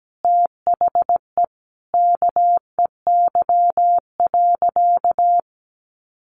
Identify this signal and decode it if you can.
Morse code, heard as a keyed tone, about 17 wpm, 700 Hz. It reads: THE KEY.
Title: Sentences from Top 600 Words 17wpm